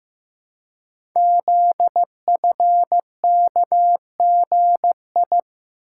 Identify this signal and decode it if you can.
Morse code, heard as a keyed tone, about 15 wpm, 700 Hz. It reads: ZFKGI
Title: Code Groups - 5 Letters - 15wpm